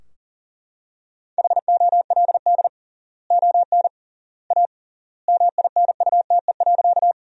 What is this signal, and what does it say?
Morse code, heard as a keyed tone, about 40 wpm, 700 Hz. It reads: HOLD ON A MINUTE.
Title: 3x Word-Spacing - Sentences from Top 500 Words - 40wpm